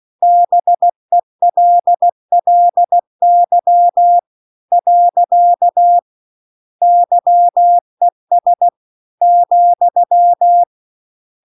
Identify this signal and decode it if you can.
Morse code, heard as a keyed tone, about 16 wpm, 700 Hz. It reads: BELLY . YES ,